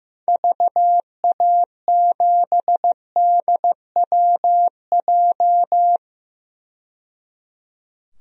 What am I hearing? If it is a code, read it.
Morse code, heard as a keyed tone, about 15 wpm, 700 Hz. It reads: VA7DWJ
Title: DXCC-100 Call Signs - 6 Characters - 15wpm